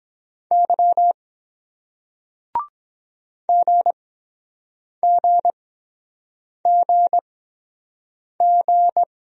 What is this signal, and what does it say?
Morse code, heard as a keyed tone, about 23 wpm, spaced as for 7 wpm, 700 Hz. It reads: Y GGGG